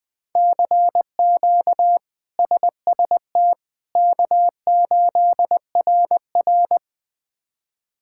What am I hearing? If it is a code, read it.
Morse code, heard as a keyed tone, about 20 wpm, 700 Hz. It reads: CQ SST K8RR
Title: K1USN SST QSO Practice - 20wpm